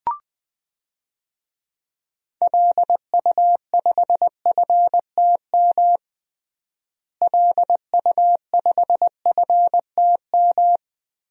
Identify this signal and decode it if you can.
Morse code, heard as a keyed tone, about 21 wpm, 700 Hz. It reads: LU5FTM LU5FTM
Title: DXCC-100 Call Signs - 6 Characters - Repeated 2x - 20wpm